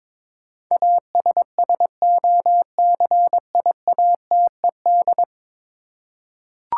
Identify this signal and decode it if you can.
Morse code, heard as a keyed tone, about 22 wpm, 700 Hz. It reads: ASSOCIATED
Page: morse-code-ninja-practice-22wpm